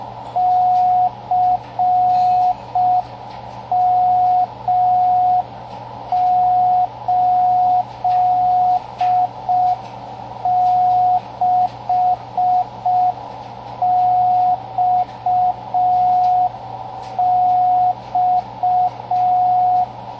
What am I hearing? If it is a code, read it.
Morse code, heard as a keyed tone, about 5 wpm, 730 Hz, with radio noise. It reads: CM86XX
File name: CWResources / fun stuff / 07 - Slow Beacon